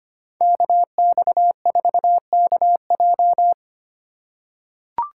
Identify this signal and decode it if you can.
Morse code, heard as a keyed tone, about 25 wpm, 700 Hz. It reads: KX4KJ
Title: USA Call Signs - 5 Characters - 2x Character Spacing - 25wpm